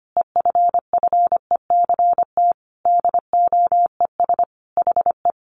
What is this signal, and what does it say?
Morse code, heard as a keyed tone, about 25 wpm, 700 Hz. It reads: EFFECT DOES HE
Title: Sentences from Top 1000 Words 25wpm